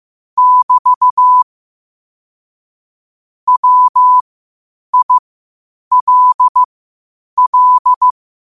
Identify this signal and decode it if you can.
Morse code, heard as a keyed tone, about 15 wpm, spaced as for 8 wpm, 1000 Hz. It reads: <BT> WILL
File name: qotdfiles / 8wpm